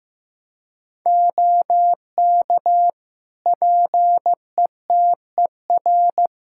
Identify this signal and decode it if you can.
Morse code, heard as a keyed tone, about 16 wpm, 700 Hz. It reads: OK PETER
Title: K1USN SST QSO Practice 15wpm